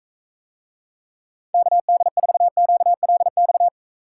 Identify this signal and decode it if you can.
Morse code, heard as a keyed tone, about 40 wpm, 700 Hz. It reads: KD4QLX